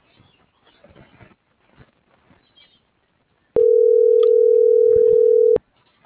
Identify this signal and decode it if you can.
Ringback tone (North America), 440 + 480 Hz.